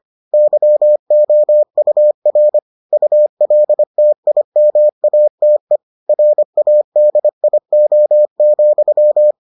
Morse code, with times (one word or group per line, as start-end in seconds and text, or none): 0.3-2.6 YOUR
2.9-5.8 ULTIMATE
6.1-9.3 RADIO,